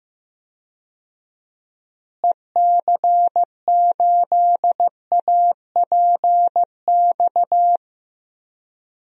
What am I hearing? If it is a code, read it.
Morse code, heard as a keyed tone, about 15 wpm, 700 Hz. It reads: EC8APX
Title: DXCC-100 Call Signs - 6 Characters - Repeated 3x - 15wpm